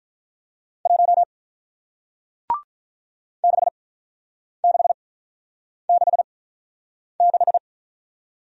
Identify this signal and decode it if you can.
Morse code, heard as a keyed tone, about 47 wpm, spaced as for 9 wpm, 700 Hz. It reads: 1 6666